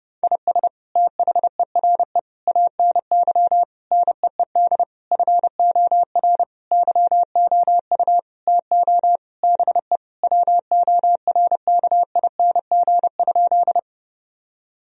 Morse code, heard as a keyed tone, about 30 wpm, 700 Hz. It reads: IS THERE ANY NEED FOR YOU TO BE WORKING?
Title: Sentences from Top 1000 Words - Review Part 4 - 30wpm